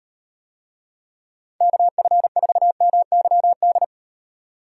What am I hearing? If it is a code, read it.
Morse code, heard as a keyed tone, about 38 wpm, 700 Hz. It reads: KF4MYD